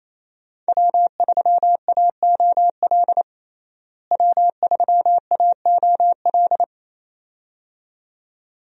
Morse code, heard as a keyed tone, about 28 wpm, 700 Hz. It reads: W3AOL W3AOL
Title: USA Call Signs - 5 Characters - Repeated 2x - 28wpm